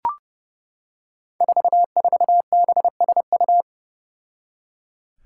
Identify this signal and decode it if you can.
Morse code, heard as a keyed tone, about 32 wpm, 700 Hz. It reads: 44BSU
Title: Five Challenging Letter-Number Combinations 30wpm